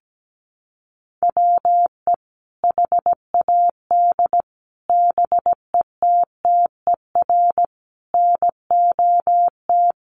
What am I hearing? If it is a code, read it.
Morse code, heard as a keyed tone, about 17 wpm, 700 Hz. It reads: WE HAD BETTER NOT
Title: Sentences from Top 1100 Words 17wpm